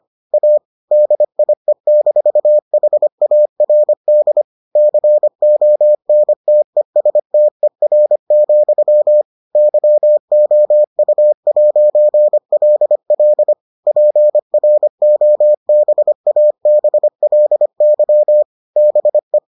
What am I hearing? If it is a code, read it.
Morse code, heard as a keyed tone, about 25 wpm, 600 Hz. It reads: A DIE-HARD CONTESTER, YOU'LL PROBABLY BE